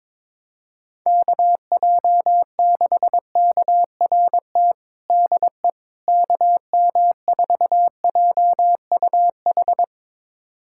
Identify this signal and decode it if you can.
Morse code, heard as a keyed tone, about 22 wpm, 700 Hz. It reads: KJ6KRT DE KM4JUH
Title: USA Callsign DE Callsign 22wpm